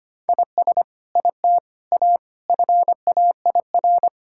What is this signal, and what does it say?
Morse code, heard as a keyed tone, about 25 wpm, 700 Hz. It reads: IS IT A FAIR
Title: Sentences from Top 1600 Words 25wpm